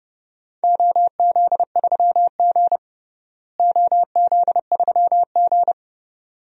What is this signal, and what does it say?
Morse code, heard as a keyed tone, about 30 wpm, 700 Hz. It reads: OZ3G OZ3G
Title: DXCC-100 Call Signs - 4 Characters - Repeated 2x - 30wpm